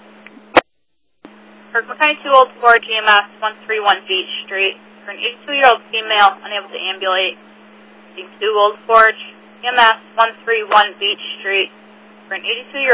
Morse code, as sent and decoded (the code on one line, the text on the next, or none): none